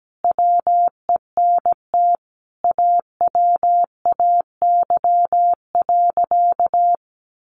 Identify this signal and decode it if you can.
Morse code, heard as a keyed tone, about 17 wpm, 700 Hz. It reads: WENT AWAY.
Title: Sentences from Top 1300 Words 17wpm